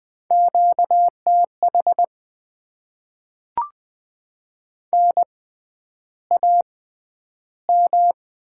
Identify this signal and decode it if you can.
Morse code, heard as a keyed tone, about 20 wpm, spaced as for 10 wpm, 700 Hz. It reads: QTH NAM
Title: Top 20 QSO Elements - 6x Character Spacing - 20wpm